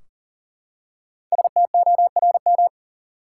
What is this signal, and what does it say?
Morse code, heard as a keyed tone, about 40 wpm, 700 Hz. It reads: STORM
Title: Five Letter Words - 40wpm